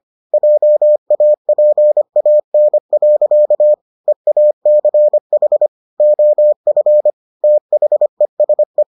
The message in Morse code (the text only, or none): JAPAN. EACH OF THESE